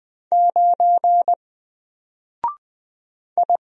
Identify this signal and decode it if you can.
Morse code, heard as a keyed tone, about 20 wpm, 700 Hz. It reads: 9 I
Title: Single Character (Letter:Number:Punctuation:Prosign) Alt 20wpm